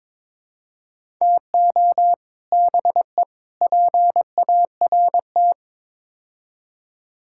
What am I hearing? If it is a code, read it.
Morse code, heard as a keyed tone, about 22 wpm, 700 Hz. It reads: TO BE PART
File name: practice / 22wpm / Build A Phrase - 5-Word Phrases - 22wpm